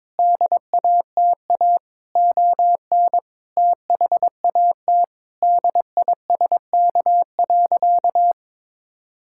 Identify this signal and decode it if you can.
Morse code, heard as a keyed tone, about 22 wpm, 700 Hz. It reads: DATA ON THAT DISK.
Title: Sentences from Top 1000 Words - Review Part 6 - 22wpm